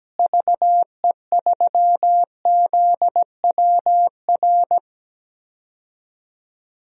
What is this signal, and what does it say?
Morse code, heard as a keyed tone, about 17 wpm, 700 Hz. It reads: VE3ZWR
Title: DXCC-100 Call Signs - 6 Characters - Repeated 3x - 17wpm